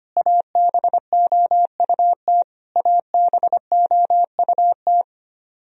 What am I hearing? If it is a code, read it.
Morse code, heard as a keyed tone, about 25 wpm, 700 Hz. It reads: ABOUT ABOUT
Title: Top 100 Words Repeated 25wpm